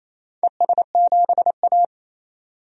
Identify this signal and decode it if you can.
Morse code, heard as a keyed tone, about 28 wpm, 700 Hz. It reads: ES7A
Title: DXCC-100 Call Signs - 4 Characters - 28wpm